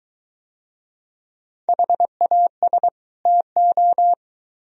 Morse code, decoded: HAS TO

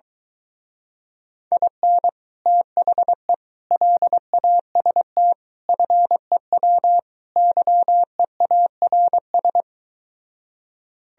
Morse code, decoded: IN THE LAST FEW YEARS

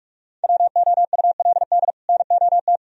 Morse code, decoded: WOULD NOT

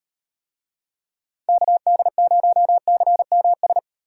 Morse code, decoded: KD0CMS

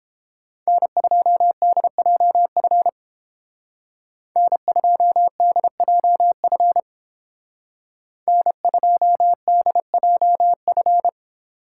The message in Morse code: N2DJF N2DJF N2DJF

